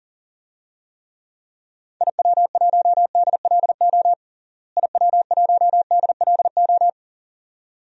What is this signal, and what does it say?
Morse code, heard as a keyed tone, about 40 wpm, 700 Hz. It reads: IW1DLO IW1DLO
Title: DXCC-100 Call Signs - 6 Characters - Repeated 2x - 40wpm